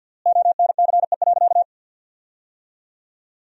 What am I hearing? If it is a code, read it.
Morse code, heard as a keyed tone, about 47 wpm, 700 Hz. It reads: ONCE.
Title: 4x Word-Spacing - Sentences from Top 300 Words - 50wpm